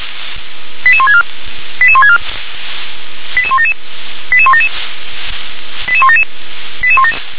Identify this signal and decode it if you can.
Five-tone selective call (ZVEI), heard as quick runs of five tones, tones of about 70 ms. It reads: 88164 88164 88177 88177 88177